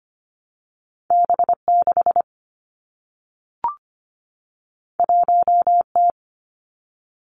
Morse code, heard as a keyed tone, about 25 wpm, 700 Hz. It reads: B 6 1 T
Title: Two Challenging Letter-Number Combinations 25wpm